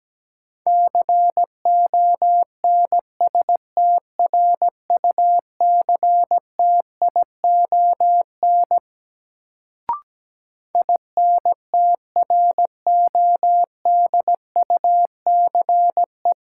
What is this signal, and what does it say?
Morse code, heard as a keyed tone, about 17 wpm, 700 Hz. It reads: CONSTRUCTION INTRODUCE
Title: Sentences from Top 1600 Words - New Word Drill Alt - 17wpm